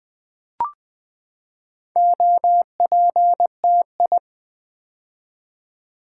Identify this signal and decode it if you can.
Morse code, heard as a keyed tone, about 20 wpm, 700 Hz. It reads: OPTI
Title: Build A Word - 4 Letters At A Time - 8-Letter Words - 20wpm